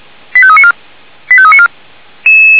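Five-tone selective call (ZVEI), heard as quick runs of five tones, tones of about 70 ms. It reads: 85394 85394
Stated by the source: ZVEI five-tone call received by SDR.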